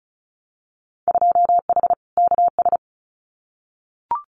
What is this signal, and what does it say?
Morse code, heard as a keyed tone, about 36 wpm, 700 Hz. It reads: 2H KS
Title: Winter Field Day Exchanges Encore 35wpm